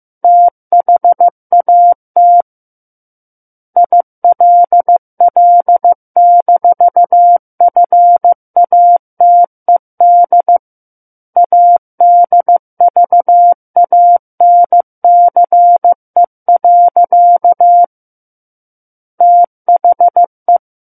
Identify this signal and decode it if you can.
Morse code, heard as a keyed tone, about 15 wpm, 700 Hz. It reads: THAT ILL-FATED ADVANCE. THE
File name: Book One / 06-Chapter 6 - 15 WPM